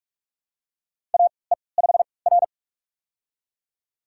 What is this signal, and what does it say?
Morse code, heard as a keyed tone, about 42 wpm, spaced as for 25 wpm, 700 Hz. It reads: AE5R
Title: USA Call Signs - 4 Characters - 3x Character Spacing - 45wpm